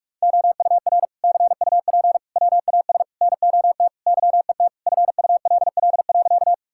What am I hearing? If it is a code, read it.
Morse code, heard as a keyed tone, about 43 wpm, 700 Hz. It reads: OUR CUP WAS NOT YET FULL.
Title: Sentences from Top 1600 Words 45wpm